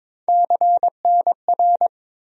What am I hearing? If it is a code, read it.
Morse code, heard as a keyed tone, about 22 wpm, 700 Hz. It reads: CNR
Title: USA Call Sign Postfixes - 3 Letters 22wpm